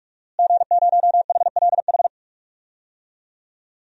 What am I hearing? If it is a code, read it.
Morse code, heard as a keyed tone, about 43 wpm, 700 Hz. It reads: G0HLH
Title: DXCC-100 Call Signs - 5 Characters - Repeated 3x - 45wpm